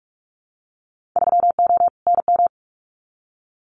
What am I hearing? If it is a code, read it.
Morse code, heard as a keyed tone, about 46 wpm, 700 Hz. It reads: 3O NM
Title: Winter Field Day Exchanges Encore 45wpm